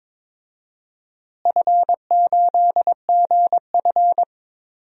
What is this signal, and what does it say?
Morse code, heard as a keyed tone, about 22 wpm, 700 Hz. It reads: F8GF
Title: DXCC-100 Call Signs - 4 Characters - Repeated 2x - 22wpm